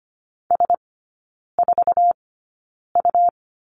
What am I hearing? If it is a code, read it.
Morse code, heard as a keyed tone, about 25 wpm, spaced as for 9 wpm, 700 Hz. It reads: S4U
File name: Lesson 17e - Letter u - Three Characters 25wpm